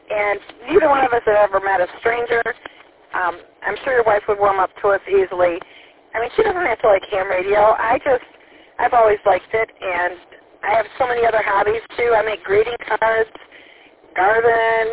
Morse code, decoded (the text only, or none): none